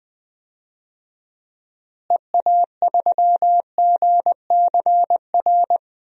Morse code, heard as a keyed tone, about 20 wpm, 700 Hz. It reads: EA3GCR
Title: DXCC-100 Call Signs - 6 Characters - Repeated 3x - 20wpm